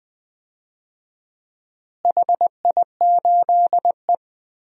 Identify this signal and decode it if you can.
Morse code, heard as a keyed tone, about 20 wpm, 700 Hz. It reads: HI8E
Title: DXCC-100 Call Signs - 4 Characters - Repeated 2x - 20wpm